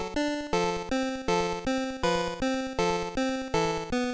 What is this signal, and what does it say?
There is no code.